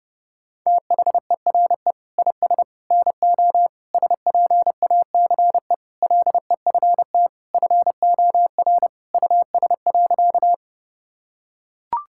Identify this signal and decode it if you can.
Morse code, heard as a keyed tone, about 32 wpm, 700 Hz. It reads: THERE IS NO SPACE LEFT FOR US.
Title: Sentences from Top 1000 Words - Review Part 5 - 30wpm